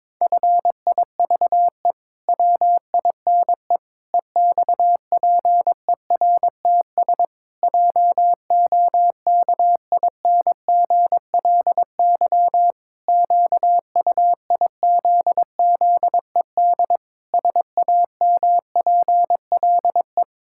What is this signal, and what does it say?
Morse code, heard as a keyed tone, about 22 wpm, 700 Hz. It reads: FIVE WINE EXPERTS JOKINGLY QUIZZED SAMPLE
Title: Pangram Challenge v2 - 22wpm